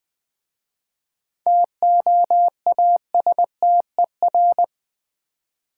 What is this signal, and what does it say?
Morse code, heard as a keyed tone, about 20 wpm, 700 Hz. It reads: TOASTER
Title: Easy Words with Hints - 20wpm